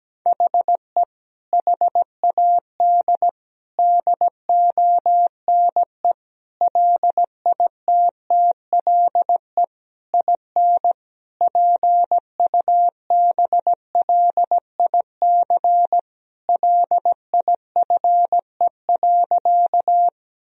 Morse code, heard as a keyed tone, about 17 wpm, 700 Hz. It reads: HE HAD DONE LITTLE IN PUBLIC LIFE.
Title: Sentences from Top 1000 Words - Review Part 3 - 17wpm